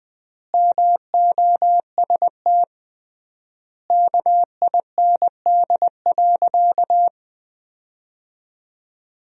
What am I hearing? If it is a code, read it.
Morse code, heard as a keyed tone, about 20 wpm, 700 Hz. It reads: MOST KIND.
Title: 3x Word-Spacing - Sentences from Top 200 Words - 20wpm